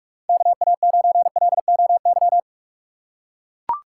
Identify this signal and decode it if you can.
Morse code, heard as a keyed tone, about 43 wpm, 690 Hz. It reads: KA9LOY
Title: USA Call Signs - 6 Characters - 3x Character Spacing - 45wpm